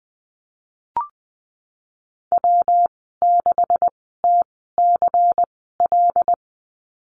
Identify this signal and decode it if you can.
Morse code, heard as a keyed tone, about 20 wpm, spaced as for 14 wpm, 700 Hz. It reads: W6TCL
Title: 128 Lesson 21g - Number 6 - Call Signs 20wpm